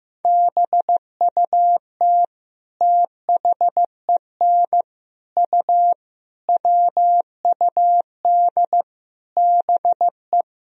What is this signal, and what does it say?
Morse code, heard as a keyed tone, about 15 wpm, 700 Hz. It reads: BUT THEN U WUD BE
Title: Sentences from Top 300 Words and Abbreviations 15wpm